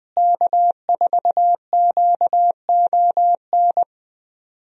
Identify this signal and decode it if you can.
Morse code, heard as a keyed tone, about 21 wpm, 700 Hz. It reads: K4QON
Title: USA Call Signs - 5 Characters - Repeated 2x - 20wpm